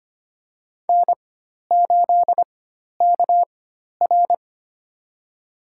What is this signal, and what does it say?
Morse code, heard as a keyed tone, about 25 wpm, spaced as for 12 wpm, 700 Hz. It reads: N8KR
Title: USA Call Signs - 4 Characters - 4x Character Spacing - 25wpm